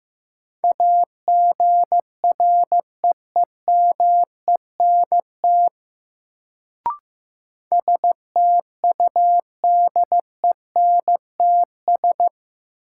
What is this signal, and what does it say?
Morse code, heard as a keyed tone, about 15 wpm, 700 Hz. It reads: AGREEMENT STUDENTS
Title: Sets of 5 Words Encore - New Word Drill - 15wpm